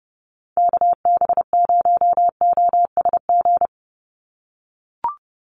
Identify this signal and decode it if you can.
Morse code, heard as a keyed tone, about 30 wpm, 700 Hz. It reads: KB0OSG
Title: USA Call Signs 30wpm